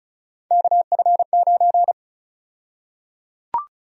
Morse code, heard as a keyed tone, about 34 wpm, 700 Hz. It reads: KF9
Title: USA Call Sign Prefixes - Repeated 2x - 35wpm